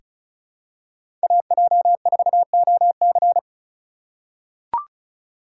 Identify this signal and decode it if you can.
Morse code, heard as a keyed tone, about 36 wpm, 700 Hz. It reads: AJ4OC
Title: USA Call Signs - 3x Character Spacing - 35wpm